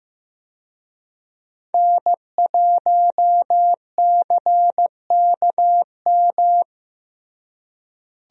Morse code, heard as a keyed tone, about 14 wpm, 700 Hz. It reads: N1CKM
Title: USA Call Signs - 5 Characters - Repeated 3x - 15wpm